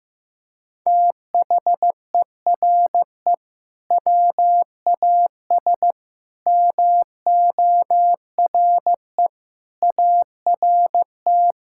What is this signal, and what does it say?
Morse code, heard as a keyed tone, about 15 wpm, 700 Hz. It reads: THERE WAS MORE ART
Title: Sentences from Top 1000 Words - Review Part 6 - 15wpm